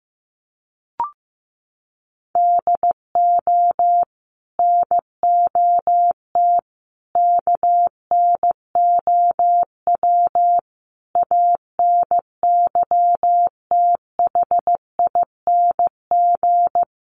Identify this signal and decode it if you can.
Morse code, heard as a keyed tone, about 15 wpm, 700 Hz. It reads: DO NOT KNOW ANYTHING
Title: Sets of 5 Words 15wpm